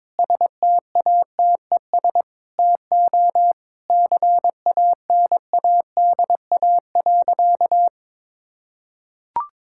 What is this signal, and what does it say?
Morse code, heard as a keyed tone, about 21 wpm, 700 Hz. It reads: STATES TO CANADA.
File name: practice / 22wpm / Sentences with 4-Word Phrases - 22wpm